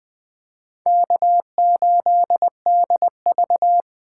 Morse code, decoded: K8DV

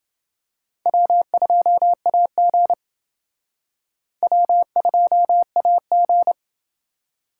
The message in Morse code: W2AG W2AG

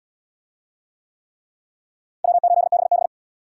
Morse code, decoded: W8BB

